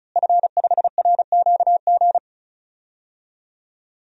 Morse code, heard as a keyed tone, about 34 wpm, 700 Hz. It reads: F5RQG
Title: DXCC-100 Call Signs - 5 Characters - Repeated 3x - 35wpm